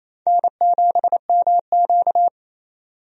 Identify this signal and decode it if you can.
Morse code, heard as a keyed tone, about 28 wpm, 700 Hz. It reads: N7MQ
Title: USA Call Signs - 4 Characters - 28wpm